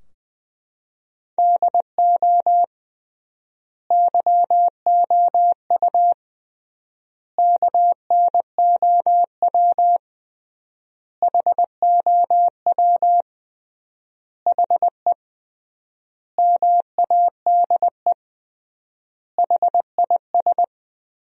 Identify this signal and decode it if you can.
Morse code, heard as a keyed tone, about 20 wpm, 700 Hz. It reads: DO YOU KNOW HOW HE MADE HIS
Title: 3x Word-Spacing - Sentences from Top 300 Words - 20wpm